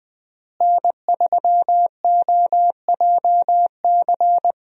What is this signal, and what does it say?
Morse code, heard as a keyed tone, about 20 wpm, 700 Hz. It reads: N3OJC
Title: USA Call Signs - 5 Characters - Repeated 2x - 20wpm